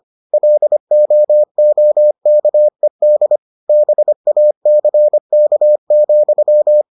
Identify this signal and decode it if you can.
Morse code, heard as a keyed tone, about 25 wpm, 600 Hz. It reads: LOOKED BACK,